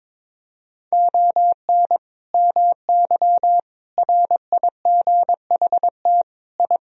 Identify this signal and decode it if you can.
Morse code, heard as a keyed tone, about 21 wpm, 700 Hz. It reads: ON MY RIGHT I
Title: Sentences from Top 1000 Words - Review Part 2 - 22wpm